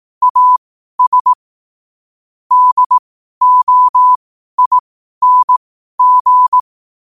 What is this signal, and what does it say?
Morse code, heard as a keyed tone, about 18 wpm, spaced as for 12 wpm, 1000 Hz. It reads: AS DOING